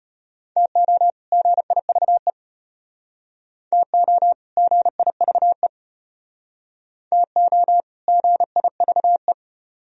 Morse code, heard as a keyed tone, about 32 wpm, 700 Hz. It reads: TO GIVE TO GIVE TO GIVE